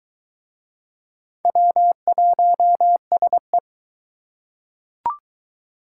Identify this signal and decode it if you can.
Morse code, heard as a keyed tone, about 20 wpm, 700 Hz. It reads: W1SE